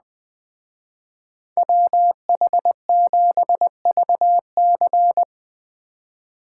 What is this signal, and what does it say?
Morse code, heard as a keyed tone, about 21 wpm, 700 Hz. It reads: WH7VC